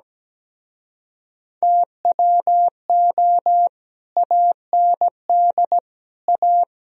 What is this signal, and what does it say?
Morse code, heard as a keyed tone, about 17 wpm, 700 Hz. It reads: TWO AND A
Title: Speed-Racing - Sets of 4 Words 17wpm